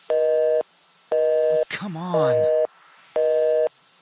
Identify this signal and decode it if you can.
Busy tone (North America), 480 + 620 Hz, with speech.
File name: phone numbers / calls / home 02 var1